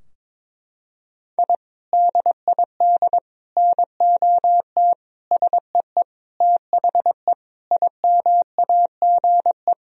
Morse code, heard as a keyed tone, about 22 wpm, 700 Hz. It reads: I DID NOT SEE THE IMAGE
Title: Sentences from Top 1000 Words - Review Part 4 - 22wpm